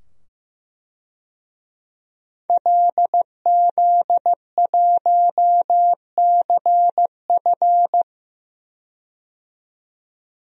Morse code, heard as a keyed tone, about 15 wpm, 700 Hz. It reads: LZ1CF
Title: DXCC-100 Call Signs - 5 Characters - 15wpm